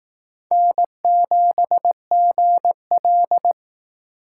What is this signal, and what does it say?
Morse code, heard as a keyed tone, about 18 wpm, 700 Hz. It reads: N7GL
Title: Speed Racing - USA 1x2 Call Signs 15wpm